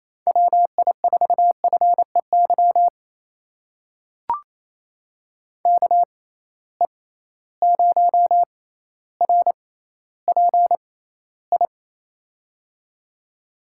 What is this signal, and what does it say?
Morse code, heard as a keyed tone, about 28 wpm, spaced as for 14 wpm, 700 Hz. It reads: WI4FEY KE0RPI
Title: USA Call Signs - 6 Characters - 6x Character Spacing - 28wpm